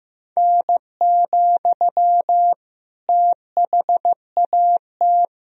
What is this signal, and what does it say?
Morse code, heard as a keyed tone, about 15 wpm, 700 Hz. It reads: N, THAT
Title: Sentences from Top 300 Words and Abbreviations 15wpm